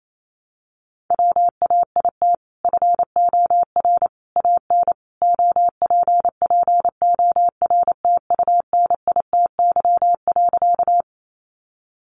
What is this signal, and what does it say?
Morse code, heard as a keyed tone, about 28 wpm, 700 Hz. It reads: WAIT FOR AN OPPORTUNITY.
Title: Sentences from Top 1000 Words 28wpm